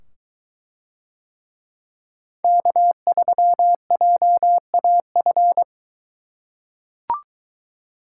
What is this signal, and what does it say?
Morse code, heard as a keyed tone, about 23 wpm, 700 Hz. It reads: K3JAF